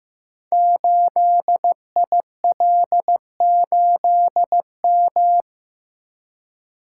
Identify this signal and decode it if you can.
Morse code, heard as a keyed tone, about 15 wpm, 700 Hz. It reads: 8IL8M
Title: Code Groups - 5 Letter:Numbers - 15wpm